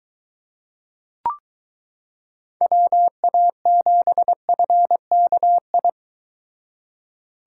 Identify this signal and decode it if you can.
Morse code, heard as a keyed tone, about 22 wpm, 700 Hz. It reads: WA7FKI